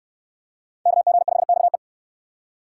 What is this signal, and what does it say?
Morse code, heard as a keyed tone, about 59 wpm, 700 Hz. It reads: KG5XE